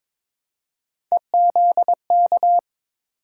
Morse code, heard as a keyed tone, about 22 wpm, 700 Hz. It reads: EZK
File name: practice / 22wpm / Three Letter-Number Combinations - 22wpm